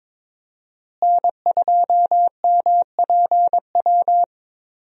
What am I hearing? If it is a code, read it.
Morse code, heard as a keyed tone, about 22 wpm, 700 Hz. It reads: N2MPW
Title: USA Call Signs - 5 Characters - Repeated 3x - 22wpm